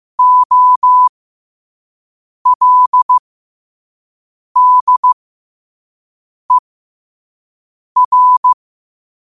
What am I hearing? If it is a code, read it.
Morse code, heard as a keyed tone, about 15 wpm, spaced as for 5 wpm, 1000 Hz. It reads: OLDER
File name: qotdfiles / 5wpm